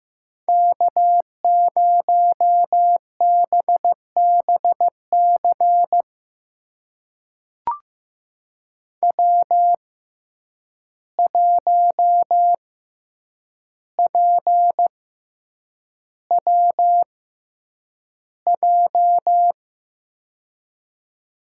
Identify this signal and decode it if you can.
Morse code, heard as a keyed tone, about 15 wpm, spaced as for 8 wpm, 700 Hz. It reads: K0BBC W1PWJ
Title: USA Call Signs - 5 Characters - 6x Character Spacing - 15wpm